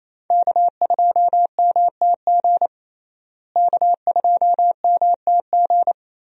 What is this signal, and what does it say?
Morse code, heard as a keyed tone, about 27 wpm, 700 Hz. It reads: K2MTG K2MTG
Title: USA Call Signs - 5 Characters - Repeated 3x - 28wpm